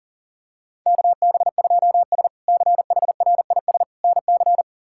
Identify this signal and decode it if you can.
Morse code, heard as a keyed tone, about 40 wpm, 700 Hz. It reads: KB2S CHRIS NC
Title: K1USN SST QSO Practice - 40wpm